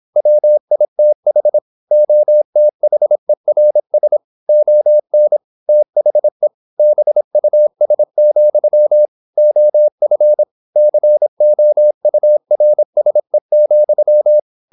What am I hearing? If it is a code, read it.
Morse code, heard as a keyed tone, about 26 wpm, 600 Hz. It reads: WITH OTHERS ON THE BUS, OF COURSE,
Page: Never 26wpm | CW med Gnister